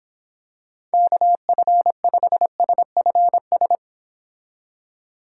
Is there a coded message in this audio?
Yes: Morse code from 0.9 to 3.8 s, heard as a keyed tone, about 26 wpm, 700 Hz. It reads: KF5SFS